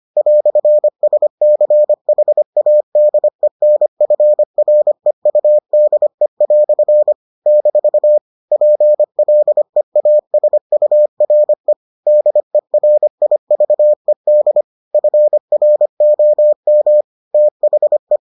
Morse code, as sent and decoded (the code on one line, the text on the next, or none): .-..-. ... -.-. .... .- -.. -. ..-. .-. ..- -.. .-..-. / -....- / .--. .-.. .- ... ..- .-. / -.. .-. .. ...- . -.. / ..-. .-. --- -- / - .... .
"SCHADENFREUDE" - PLEASURE DERIVED FROM THE